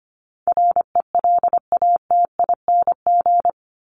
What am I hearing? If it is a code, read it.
Morse code, heard as a keyed tone, about 25 wpm, 700 Hz. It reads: RELATING
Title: ING Suffix - Top 100 Words with Base Words 25wpm